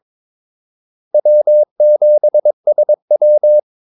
Morse code, heard as a keyed tone, about 22 wpm, 600 Hz. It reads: W7SW
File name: Callsigns-22wpm